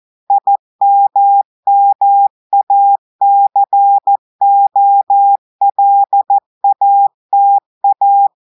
Morse code, heard as a keyed tone, about 14 wpm, 800 Hz. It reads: IMMACOLATA
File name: audios / portugais / portugais texte en clair 1 vitesse 14 mots minutes